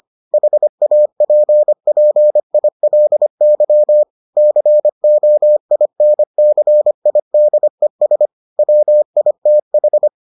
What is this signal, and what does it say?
Morse code, heard as a keyed tone, about 25 wpm, 600 Hz. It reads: HAPPILY COINCIDES WITH